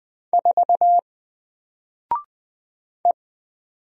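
Morse code, heard as a keyed tone, about 20 wpm, 700 Hz. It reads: 4 E
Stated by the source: Single letter and number Morse drill at 20 wpm, from N4LCD.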